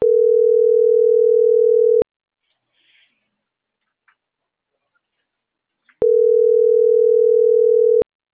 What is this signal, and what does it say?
Ringback tone (North America), 440 + 480 Hz.